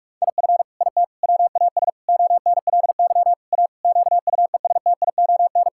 Morse code, heard as a keyed tone, about 41 wpm, 700 Hz. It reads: IF IT WAS ONLY A QUESTION
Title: Sentences from Top 1600 Words 45wpm